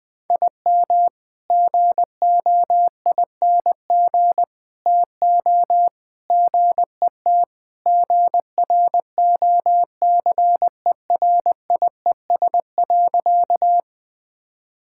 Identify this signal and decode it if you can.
Morse code, heard as a keyed tone, about 20 wpm, 700 Hz. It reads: IM GOING TO GET GROCERIES.
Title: Sentences with 3-Word Phrases - Spoken Beginning - 20wpm